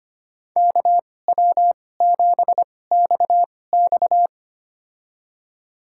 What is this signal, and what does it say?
Morse code, heard as a keyed tone, about 25 wpm, spaced as for 18 wpm, 700 Hz. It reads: KW7XX
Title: USA Call Signs - 5 Characters - 2x Character Spacing - 25wpm